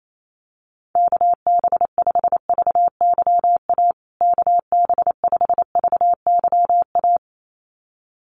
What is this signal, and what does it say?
Morse code, heard as a keyed tone, about 28 wpm, 700 Hz. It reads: KB5VYA KB5VYA
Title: USA Call Signs Repeated 2x 28wpm